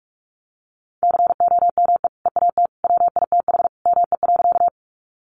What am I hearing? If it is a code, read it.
Morse code, heard as a keyed tone, about 44 wpm, 700 Hz. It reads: COME EAT WITH ME.
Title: Sentences from Top 300 Words 45wpm